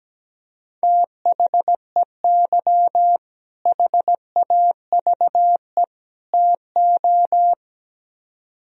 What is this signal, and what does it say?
Morse code, heard as a keyed tone, about 17 wpm, 700 Hz. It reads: THEY HAVE TO
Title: Speed-Racing - Sets of 3 Words 17wpm